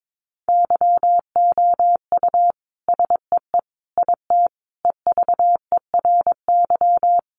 Morse code, heard as a keyed tone, about 22 wpm, 700 Hz. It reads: YOU SEE IT EVERY
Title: Sentences from Top 200 Words 22wpm